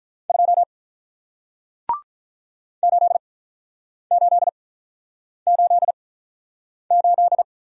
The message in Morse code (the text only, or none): 2 8888